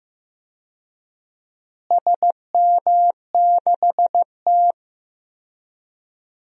Morse code, heard as a keyed tone, about 15 wpm, 700 Hz. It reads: SM6T
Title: DXCC-100 Call Signs - 4 Characters - Repeated 3x - 15wpm